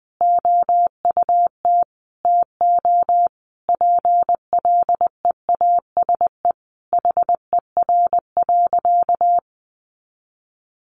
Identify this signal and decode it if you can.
Morse code, heard as a keyed tone, about 20 wpm, 700 Hz. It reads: OUT TO PLEASE HER.
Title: Sentences from Top 1500 Words 20wpm